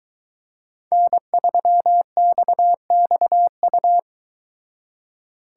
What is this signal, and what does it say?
Morse code, heard as a keyed tone, about 22 wpm, 700 Hz. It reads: N3XXU